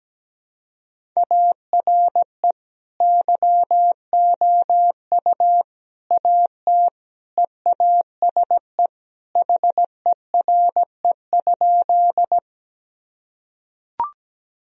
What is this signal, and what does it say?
Morse code, heard as a keyed tone, about 17 wpm, 700 Hz. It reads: ARE YOU AT EASE HERE?
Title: Sentences from Top 1000 Words - Review Part 1 - 17wpm